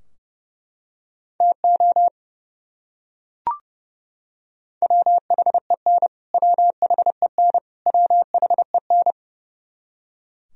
Morse code, heard as a keyed tone, about 32 wpm, 700 Hz. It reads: TO WHEN WHEN WHEN
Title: Top 100 Words Repeated 30wpm